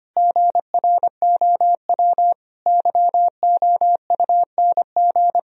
Morse code, heard as a keyed tone, about 25 wpm, 700 Hz. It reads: GROW YOUNG
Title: Sentences from Top 1000 Words - Review Part 1 - 25wpm